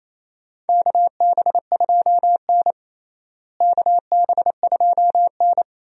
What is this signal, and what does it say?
Morse code, heard as a keyed tone, about 27 wpm, 700 Hz. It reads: KB2N KB2N
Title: USA Call Signs - 4 Characters - Repeated 3x - 28wpm